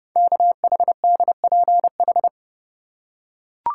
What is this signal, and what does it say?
Morse code, heard as a keyed tone, about 30 wpm, 700 Hz. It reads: KHDPH
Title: Code Groups - 5 Letters - 30wpm